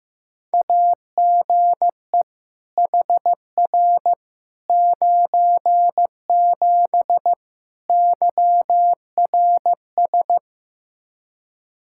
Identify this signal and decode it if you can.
Morse code, heard as a keyed tone, about 16 wpm, 700 Hz. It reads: AGE HR 97 YRS
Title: Practice QSO - Exchange 3 - More Info - 15wpm